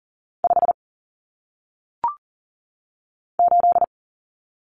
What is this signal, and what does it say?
Morse code, heard as a keyed tone, about 39 wpm, 700 Hz. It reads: 5 8
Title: Numbers - 1-Digit 40wpm